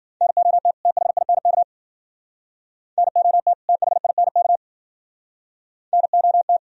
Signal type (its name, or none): Morse code